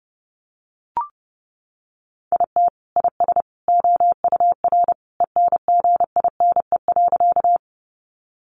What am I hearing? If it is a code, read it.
Morse code, heard as a keyed tone, about 30 wpm, 700 Hz. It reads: IT IS OUR ENGINE.
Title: Sentences from Top 1500 Words 30wpm